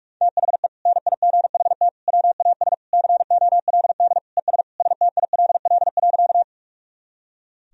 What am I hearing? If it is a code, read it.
Morse code, heard as a keyed tone, about 45 wpm, 700 Hz. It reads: THE NIGHT WAS COLD ES STILL.
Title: Sentences from Top 400 Words and Abbreviations 45wpm